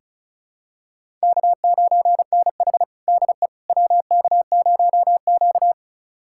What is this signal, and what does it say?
Morse code, heard as a keyed tone, about 36 wpm, 700 Hz. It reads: K9NH DE WK0Q
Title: USA Callsign DE Callsign - 4 Characters - 35wpm